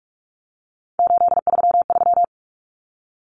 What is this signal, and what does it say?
Morse code, heard as a keyed tone, about 44 wpm, 700 Hz. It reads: 833
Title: Common Cut Numbers - 3-Digit 45wpm